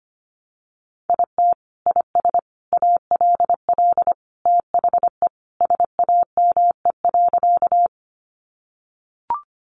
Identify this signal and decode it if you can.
Morse code, heard as a keyed tone, about 25 wpm, 700 Hz. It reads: IT IS ALL THE SAME.
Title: Sentences from Top 200 Words 25wpm